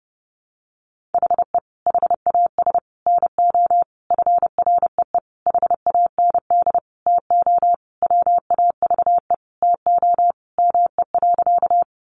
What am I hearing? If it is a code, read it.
Morse code, heard as a keyed tone, about 30 wpm, 700 Hz. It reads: HE HAS NO FREE HAND TO WAVE TO ME.
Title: Sentences from Top 1400 Words 30wpm